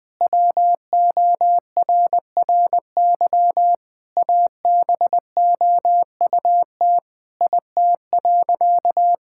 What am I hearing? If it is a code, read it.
Morse code, heard as a keyed tone, about 20 wpm, 700 Hz. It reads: WORRY ABOUT IT.
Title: Sentences with 6-Word Phrases - 20wpm